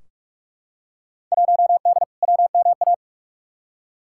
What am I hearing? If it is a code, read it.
Morse code, heard as a keyed tone, about 43 wpm, 700 Hz. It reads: 1D WMA
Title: Field Day Exchanges - 45wpm